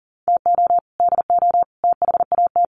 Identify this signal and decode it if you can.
Morse code, heard as a keyed tone, about 40 wpm, 700 Hz. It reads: TO DO THAT
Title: Sets of 3 Words 40wpm